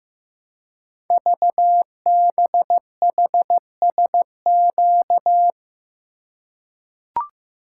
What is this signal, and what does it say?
Morse code, heard as a keyed tone, about 15 wpm, 700 Hz. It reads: VBHSQ
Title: Code Groups - 5 Letter:Numbers - 15wpm